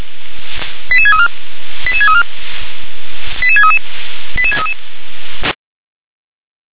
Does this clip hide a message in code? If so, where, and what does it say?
Yes: five-tone selective call (ZVEI) from 0.9 to 4.7 s, heard as quick runs of five tones, tones of about 70 ms. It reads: 88634 88634 88633 88633